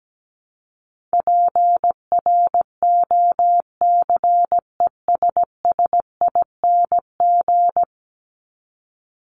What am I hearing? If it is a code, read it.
Morse code, heard as a keyed tone, about 17 wpm, 700 Hz. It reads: PROCESSING
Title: ING Suffix - Top 100 Words with Base Words 17wpm